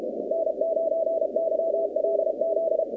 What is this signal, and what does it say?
Morse code, heard as a keyed tone, about 33 wpm, 650 Hz, with noise. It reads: N9YLZ
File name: 2019-cw-wpx-kt5j-k5pi